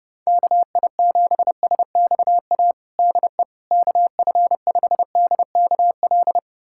Morse code, heard as a keyed tone, about 30 wpm, 700 Hz. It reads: KI7SXA DE KF5DKL
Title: USA Callsign DE Callsign - 6 Characters - 30wpm